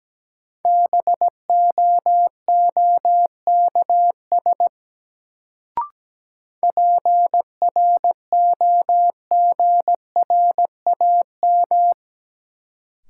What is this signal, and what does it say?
Morse code, heard as a keyed tone, about 17 wpm, 700 Hz. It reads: BOOKS PROGRAM
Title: Sentences from Top 1000 Words - Review Part 4 - New Word Drill - 17wpm